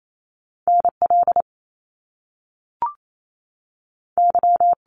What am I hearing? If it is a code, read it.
Morse code, heard as a keyed tone, about 28 wpm, spaced as for 6 wpm, 700 Hz. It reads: NLY
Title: Canadian Provinces and Territories 28wpm